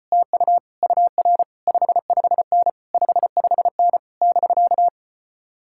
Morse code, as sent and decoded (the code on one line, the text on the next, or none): - ..- / ..- .-. / ..... ..... -. / ..... ..... -. / -...-.-
TU UR 55N 55N <BK>